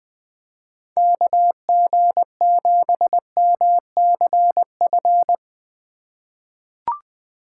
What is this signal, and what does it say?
Morse code, heard as a keyed tone, about 20 wpm, 700 Hz. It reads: KG7MCF
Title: USA Call Signs - 6 Characters - 3x Character Spacing - 20wpm